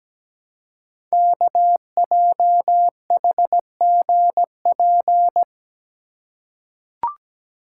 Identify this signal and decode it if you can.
Morse code, heard as a keyed tone, about 17 wpm, 700 Hz. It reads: KJHGP
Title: Code Groups - 5 Letters - 17wpm